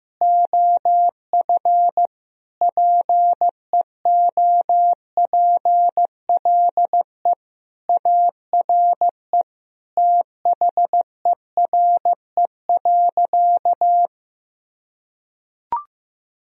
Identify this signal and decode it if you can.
Morse code, heard as a keyed tone, about 15 wpm, 700 Hz. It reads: OF PEOPLE ARE THERE.
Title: Sentences from Top 1000 Words - Review Part 1 - 15wpm